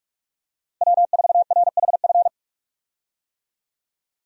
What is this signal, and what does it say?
Morse code, heard as a keyed tone, about 44 wpm, 700 Hz. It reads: W4RHF
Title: USA Call Signs - 5 Characters - Repeated 3x - 45wpm